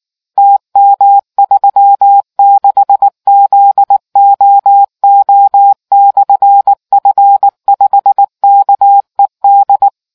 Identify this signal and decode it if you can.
Morse code, heard as a keyed tone, about 19 wpm, 800 Hz. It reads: TM36ZOO/F5KED